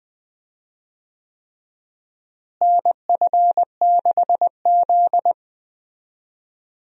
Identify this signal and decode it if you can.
Morse code, heard as a keyed tone, about 20 wpm, 700 Hz. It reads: NF6Z
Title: USA Call Signs - 4 Characters - Repeated 3x - 20wpm